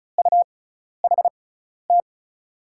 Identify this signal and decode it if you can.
Morse code, heard as a keyed tone, about 36 wpm, spaced as for 12 wpm, 700 Hz. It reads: UHT
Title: USA Call Signs - 6x Character Spacing - 35wpm